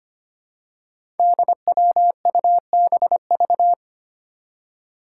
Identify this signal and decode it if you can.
Morse code, heard as a keyed tone, about 25 wpm, 700 Hz. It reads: DWUBV
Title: Five Challenging Letter-Number Combinations 25wpm